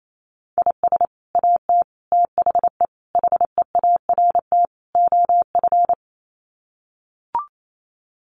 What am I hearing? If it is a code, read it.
Morse code, heard as a keyed tone, about 28 wpm, 700 Hz. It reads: IS AT THE HEART OF
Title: Sets of 5 Words Encore2 28wpm